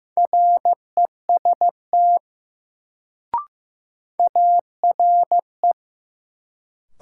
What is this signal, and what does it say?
Morse code, heard as a keyed tone, about 15 wpm, 700 Hz. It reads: REST ARE
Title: Sets of 2 Words - New Word Drill - 15wpm